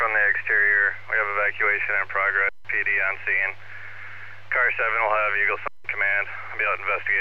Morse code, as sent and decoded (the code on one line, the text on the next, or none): none